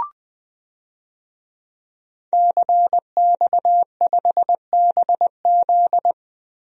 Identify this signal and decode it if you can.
Morse code, heard as a keyed tone, about 20 wpm, 700 Hz. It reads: CX5BZ
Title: DXCC-100 Call Signs - 5 Characters - Repeated 2x - 20wpm